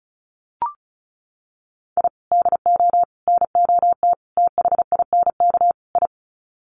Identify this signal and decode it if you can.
Morse code, heard as a keyed tone, about 34 wpm, 700 Hz. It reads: I DO NOT THINK I
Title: Sets of 5 Words 35wpm